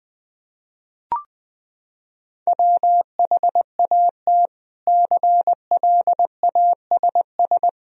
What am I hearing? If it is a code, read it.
Morse code, heard as a keyed tone, about 20 wpm, 700 Hz. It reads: WHAT CLASS
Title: Sentences from Top 1000 Words - Review Part 2 - 20wpm